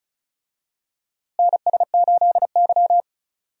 Morse code, heard as a keyed tone, about 34 wpm, 700 Hz. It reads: NS8Y
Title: USA Call Signs - 4 Characters - Repeated 2x - 35wpm